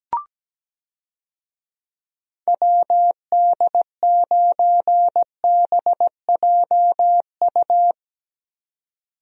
Morse code, heard as a keyed tone, about 17 wpm, 700 Hz. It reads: WD9BJU
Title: USA Call Signs - 6 Characters - 17wpm